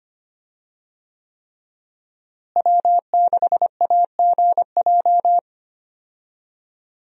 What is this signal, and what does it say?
Morse code, heard as a keyed tone, about 25 wpm, 700 Hz. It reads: W6AGJ
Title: USA Call Signs - 5 Characters - Repeated 3x - 25wpm